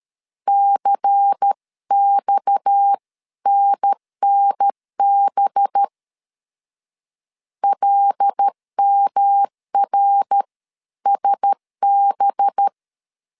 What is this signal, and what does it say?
Morse code, heard as a keyed tone, about 13 wpm, 800 Hz. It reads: CXNNB LMRSB
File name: CW / DET / 1 6 1500C S A 12wpm 3440